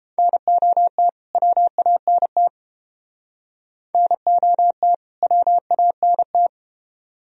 Morse code, heard as a keyed tone, about 31 wpm, 700 Hz. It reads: NOT WANT NOT WANT